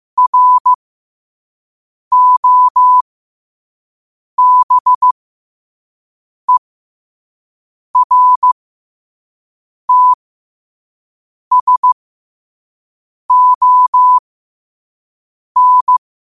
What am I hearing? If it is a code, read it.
Morse code, heard as a keyed tone, about 15 wpm, spaced as for 5 wpm, 1000 Hz. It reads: ROBERTSON
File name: qotdfiles / 5wpm